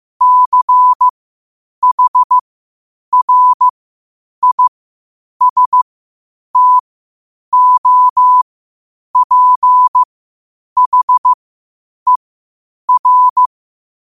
Morse code, heard as a keyed tone, about 15 wpm, spaced as for 8 wpm, 1000 Hz. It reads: CHRISTOPHER
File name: qotdfiles / 8wpm